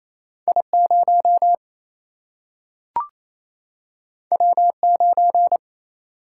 Morse code, heard as a keyed tone, about 28 wpm, 700 Hz. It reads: I0 W9
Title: Single-Letter Single-Number 28wpm